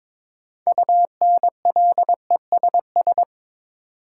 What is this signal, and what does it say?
Morse code, heard as a keyed tone, about 22 wpm, 700 Hz. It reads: UNLESS
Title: Sentences from Top 1000 Words - Review Part 5 - New Word Drill - 22wpm